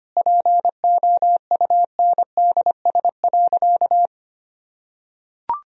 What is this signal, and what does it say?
Morse code, heard as a keyed tone, about 25 wpm, 700 Hz. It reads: POUNDS.
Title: Sentences from Top 1000 Words - Review Part 2 - 25wpm